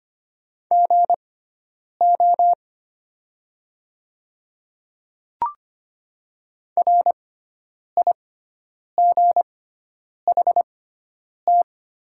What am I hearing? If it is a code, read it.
Morse code, heard as a keyed tone, about 25 wpm, spaced as for 9 wpm, 700 Hz. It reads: GO RIGHT
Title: 6x Character Spacing - Top 200 Words - 25wpm